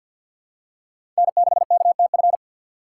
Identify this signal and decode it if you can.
Morse code, heard as a keyed tone, about 46 wpm, 690 Hz. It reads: N6KTF
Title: USA Call Signs - 6x Character Spacing - 50wpm